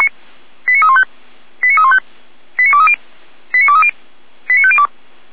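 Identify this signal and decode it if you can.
Five-tone selective call (ZVEI), heard as quick runs of five tones, tones of about 70 ms. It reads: 89315 89315 89230 89230 89502
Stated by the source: ZVEI five-tone selective call received on SDR.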